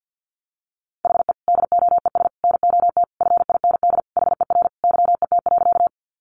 Morse code, heard as a keyed tone, about 50 wpm, 700 Hz. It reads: HE DOES NOT FIND HER YET.
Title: Easy Phrases 50wpm